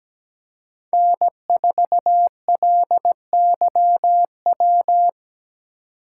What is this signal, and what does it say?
Morse code, heard as a keyed tone, about 17 wpm, 700 Hz. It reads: N4LYW
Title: USA Call Signs - 8x Character Spacing - 17wpm